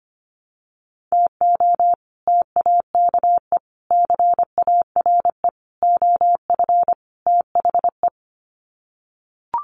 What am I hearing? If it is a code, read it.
Morse code, heard as a keyed tone, about 25 wpm, 700 Hz. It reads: TO TAKE CARE OF THE
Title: Sets of 5 Words Encore 25wpm